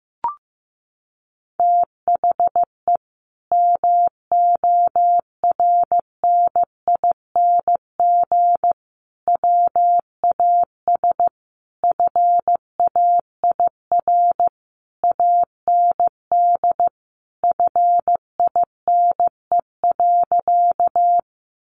Morse code, heard as a keyed tone, about 15 wpm, 700 Hz. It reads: THE MORNING WAS FAIR AND FINE.
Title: Sentences from Top 1300 Words 15wpm